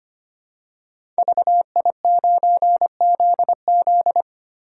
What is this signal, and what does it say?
Morse code, heard as a keyed tone, about 25 wpm, 700 Hz. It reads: VI9ZZ
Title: Code Groups - 5 Letter:Numbers - 25wpm